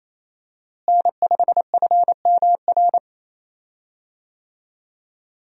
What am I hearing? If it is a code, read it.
Morse code, heard as a keyed tone, about 28 wpm, 700 Hz. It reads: N5FMR
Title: USA Call Signs - 5 Characters - Repeated 3x - 28wpm